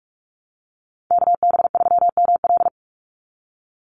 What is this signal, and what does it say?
Morse code, heard as a keyed tone, about 44 wpm, 700 Hz. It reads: KB3ML
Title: Call Signs Repeated 2x - 45wpm